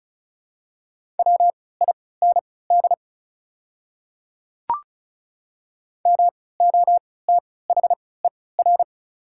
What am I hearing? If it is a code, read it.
Morse code, heard as a keyed tone, about 36 wpm, spaced as for 20 wpm, 690 Hz. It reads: WIND MOTHER
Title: 3x Character Spacing - Top 500 Words - 35wpm